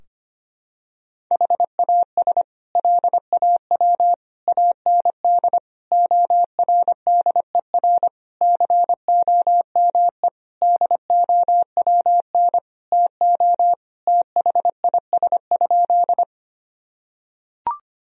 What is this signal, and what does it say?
Morse code, heard as a keyed tone, about 25 wpm, 700 Hz. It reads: HAS LAW AND ORDER COME DOWN TO THIS?
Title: Sentences from Top 1000 Words - Review Part 3 - 25wpm